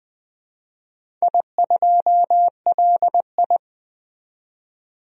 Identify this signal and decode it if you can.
Morse code, heard as a keyed tone, about 20 wpm, 700 Hz. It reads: I2LI
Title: DXCC-100 Call Signs - 4 Characters - 20wpm